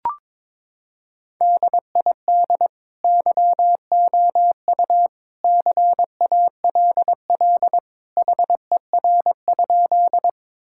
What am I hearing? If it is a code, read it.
Morse code, heard as a keyed tone, about 21 wpm, 700 Hz. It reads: DID YOU CALL HER?
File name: practice / 22wpm / Sentences with 2-Word Phrases - 22wpm